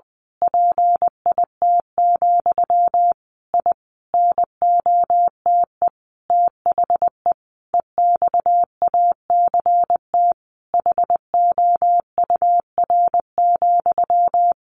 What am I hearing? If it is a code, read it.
Morse code, heard as a keyed tone, about 20 wpm, 700 Hz. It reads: PIT, I NOTE THE EXACT HOUR,
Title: A Sound Of Thunder 20wpm